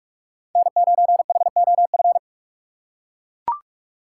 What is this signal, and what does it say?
Morse code, heard as a keyed tone, about 41 wpm, 700 Hz. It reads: N9HOF
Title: USA Call Signs - 5 Characters - 4x Character Spacing - 45wpm